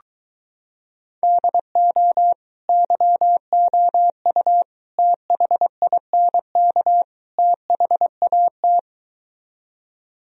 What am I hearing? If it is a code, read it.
Morse code, heard as a keyed tone, about 22 wpm, 700 Hz. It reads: DO YOU THINK THAT